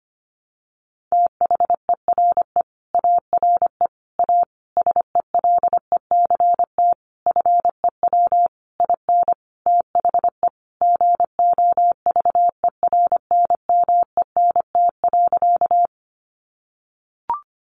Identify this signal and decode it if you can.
Morse code, heard as a keyed tone, about 25 wpm, 700 Hz. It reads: THERE ARE A SELECT FEW IN THE GOVERNMENT.
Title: Sentences from Top 1300 Words 25wpm